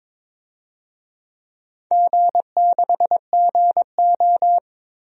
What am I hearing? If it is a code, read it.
Morse code, heard as a keyed tone, about 22 wpm, 700 Hz. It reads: G6GO
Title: DXCC-100 Call Signs - 4 Characters - Repeated 3x - 22wpm